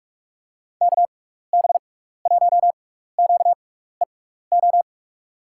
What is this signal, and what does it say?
Morse code, heard as a keyed tone, about 44 wpm, spaced as for 15 wpm, 700 Hz. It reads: KB1QEO
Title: USA Call Signs - 6 Characters - 6x Character Spacing - 45wpm